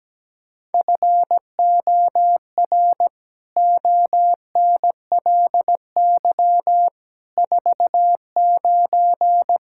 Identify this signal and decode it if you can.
Morse code, heard as a keyed tone, about 17 wpm, 700 Hz. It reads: FOR ONLY 49
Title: Pangram Challenge v2 - 17wpm